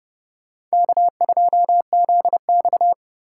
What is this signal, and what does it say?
Morse code, heard as a keyed tone, about 38 wpm, 700 Hz. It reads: K2ZX